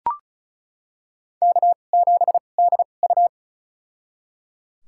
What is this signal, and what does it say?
Morse code, heard as a keyed tone, about 36 wpm, spaced as for 25 wpm, 700 Hz. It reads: K7DU
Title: USA Call Signs - 4 Characters - 2x Character Spacing - 35wpm